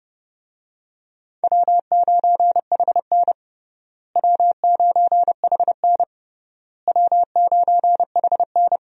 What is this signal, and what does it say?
Morse code, heard as a keyed tone, about 29 wpm, 700 Hz. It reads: W9HN W9HN W9HN
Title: USA Call Signs - 4 Characters - Repeated 3x - 30wpm